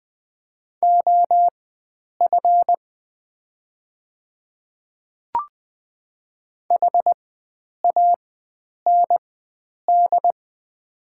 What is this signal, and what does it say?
Morse code, heard as a keyed tone, about 20 wpm, spaced as for 9 wpm, 700 Hz. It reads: OF HAND
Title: 4x Character Spacing - Top 500 Words - 20wpm